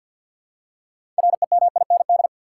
Morse code, heard as a keyed tone, about 53 wpm, 700 Hz. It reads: REMIND
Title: Six Letter Words - 50wpm